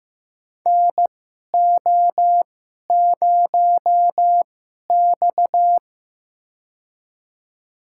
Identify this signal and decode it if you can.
Morse code, heard as a keyed tone, about 15 wpm, spaced as for 11 wpm, 700 Hz. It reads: NO0X
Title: USA Call Signs - 4 Characters - 2x Character Spacing - 15wpm